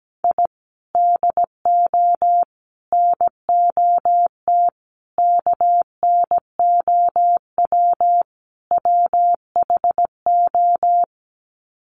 Morse code, heard as a keyed tone, about 17 wpm, 700 Hz. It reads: I DO NOT KNOW WHO
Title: Sets of 5 Words 17wpm